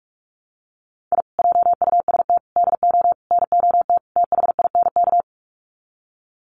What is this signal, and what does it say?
Morse code, heard as a keyed tone, about 44 wpm, 700 Hz. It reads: I JUST DO NOT THINK
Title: Sets of 5 Words Encore 45wpm